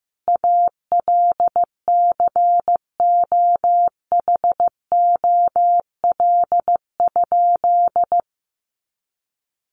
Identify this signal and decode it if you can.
Morse code, heard as a keyed tone, about 15 wpm, 700 Hz. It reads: ALCOHOL?
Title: Sentences from Top 1400 Words 15wpm